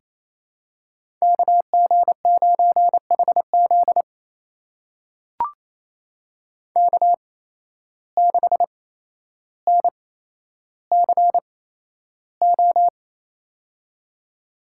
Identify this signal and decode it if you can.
Morse code, heard as a keyed tone, about 28 wpm, spaced as for 12 wpm, 700 Hz. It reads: KG9HZ K6NCO
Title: USA Call Signs - 5 Characters - 8x Character Spacing - 28wpm